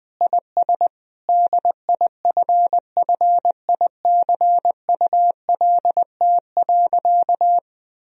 Morse code, heard as a keyed tone, about 20 wpm, 700 Hz. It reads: IS DIFFICULT.
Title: Sentences from Top 1000 Words - Review Part 3 - 20wpm